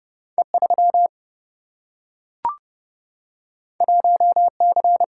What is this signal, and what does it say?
Morse code, heard as a keyed tone, about 30 wpm, 700 Hz. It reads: E3 1C
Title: Single-Letter Single-Number 30wpm